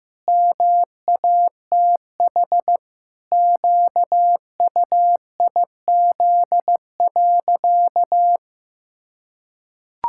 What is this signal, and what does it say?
Morse code, heard as a keyed tone, about 16 wpm, 700 Hz. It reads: MATH QUIZ.
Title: Pangram Challenge v2 - 15wpm